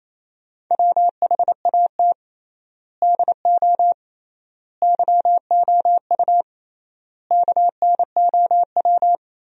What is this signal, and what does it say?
Morse code, heard as a keyed tone, about 28 wpm, 700 Hz. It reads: WHAT DO YOU KNOW
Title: 3x Word-Spacing - Sentences from Top 100 Words - 28wpm